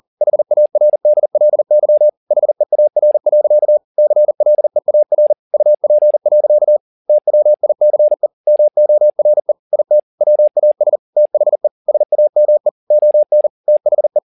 Morse code, heard as a keyed tone, about 40 wpm, 600 Hz. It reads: HARDLY HEAR. CLEAR UP. TWICE MORE IT WAS THE SAME ON THE
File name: the old man and the sea 2 40wpm